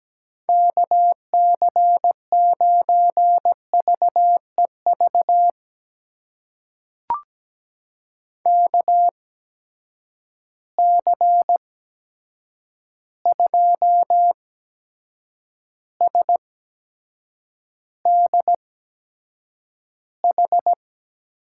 Morse code, heard as a keyed tone, about 17 wpm, spaced as for 7 wpm, 700 Hz. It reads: KC9VEV KC2SDH
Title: USA Call Signs - 6 Characters - 8x Character Spacing - 17wpm